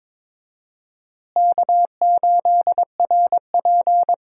Morse code, heard as a keyed tone, about 22 wpm, 700 Hz. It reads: K8RP
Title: USA Call Signs - 4 Characters - Repeated 3x - 22wpm